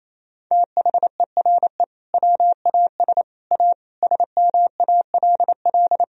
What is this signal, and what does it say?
Morse code, heard as a keyed tone, about 28 wpm, 700 Hz. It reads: THERE WAS A SMALL
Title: Sentences from Top 1000 Words - Review Part 1 - 28wpm